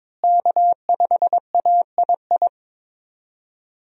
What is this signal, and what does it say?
Morse code, heard as a keyed tone, about 21 wpm, 700 Hz. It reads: K5AII
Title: USA Call Signs - 5 Characters - Repeated 2x - 22wpm